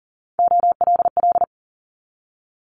Morse code, heard as a keyed tone, about 40 wpm, 700 Hz. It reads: OLL
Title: Multi-letter Phonemes 40wpm